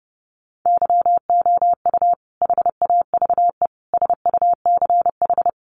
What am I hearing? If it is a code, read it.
Morse code, heard as a keyed tone, about 30 wpm, 700 Hz. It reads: YOU HAVE SUCH
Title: Sentences from Top 1200 Words 30wpm